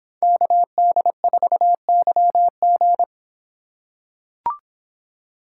Morse code, heard as a keyed tone, about 25 wpm, 700 Hz. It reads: KD4YG